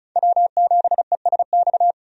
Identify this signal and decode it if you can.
Morse code, heard as a keyed tone, about 36 wpm, 700 Hz. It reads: W7ESX
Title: USA Call Signs - 5 Characters - 35wpm